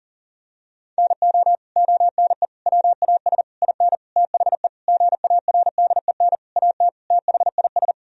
Morse code, heard as a keyed tone, about 37 wpm, 700 Hz. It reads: NO ONE WAS IN THE GARDEN AT THIS
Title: Sentences from Top 1000 Words - Review Part 4 - 40wpm